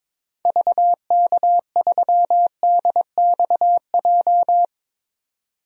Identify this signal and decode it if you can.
Morse code, heard as a keyed tone, about 22 wpm, 700 Hz. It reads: VK3DXJ
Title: DXCC-100 Call Signs - 6 Characters - 22wpm